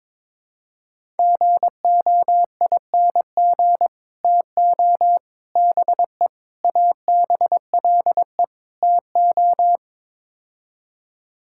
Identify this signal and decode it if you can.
Morse code, heard as a keyed tone, about 22 wpm, 700 Hz. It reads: GOING TO BE ABLE TO
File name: practice / 22wpm / Build A Phrase - 5-Word Phrases - 22wpm